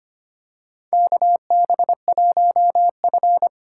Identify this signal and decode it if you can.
Morse code, heard as a keyed tone, about 25 wpm, 700 Hz. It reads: KB1F
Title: USA Call Signs - 4 Characters - 3x Character Spacing - 25wpm